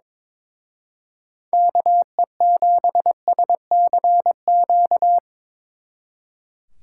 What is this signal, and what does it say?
Morse code, heard as a keyed tone, about 24 wpm, 700 Hz. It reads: KE7SCQ
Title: Speed Racing - USA 2x3 Call Signs 22wpm